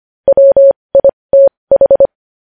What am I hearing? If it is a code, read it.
Morse code, heard as a keyed tone, about 25 wpm, spaced as for 20 wpm, 550 Hz. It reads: WITH
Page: news-headlines-in-morse-code-at-20-wpm